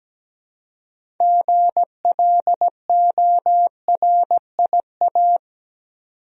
Morse code, heard as a keyed tone, about 17 wpm, 700 Hz. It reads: GLORIA
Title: USA Common Names - 17wpm